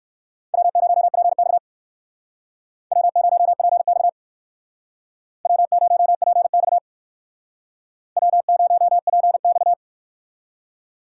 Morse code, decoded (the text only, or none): W0PX W0PX W0PX W0PX